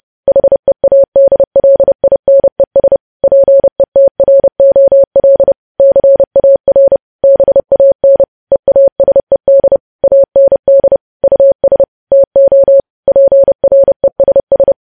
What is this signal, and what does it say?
Morse code, heard as a keyed tone, about 30 wpm, 550 Hz. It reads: HEADLINES PETROL CAR BAN EASED AND US TO PRESS
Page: news headlines in morse code at 30 wpm